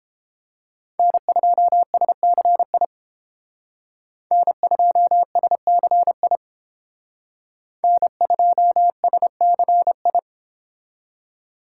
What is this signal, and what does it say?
Morse code, heard as a keyed tone, about 28 wpm, 700 Hz. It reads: N2SCI N2SCI N2SCI